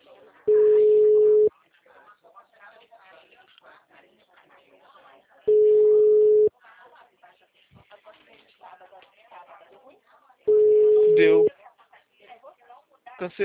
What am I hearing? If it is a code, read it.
Ringback tone (Europe, 425 Hz), with speech.